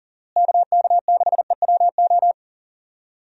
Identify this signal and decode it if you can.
Morse code, heard as a keyed tone, about 37 wpm, 700 Hz. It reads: KK6EWO
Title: USA Call Signs - 6 Characters - Repeated 2x - 40wpm